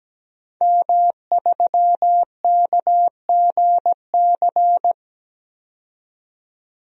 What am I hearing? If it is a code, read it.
Morse code, heard as a keyed tone, about 17 wpm, 700 Hz. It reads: M3KGC
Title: DXCC-100 Call Signs - 5 Characters - 17wpm